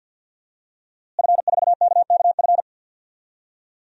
Morse code, heard as a keyed tone, about 50 wpm, 700 Hz. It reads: F4KKF
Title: DXCC-100 Call Signs - 5 Characters - Repeated 2x - 50wpm